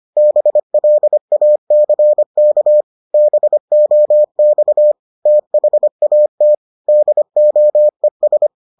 Morse code, heard as a keyed tone, about 25 wpm, 600 Hz. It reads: BLACK BOX THAT DOES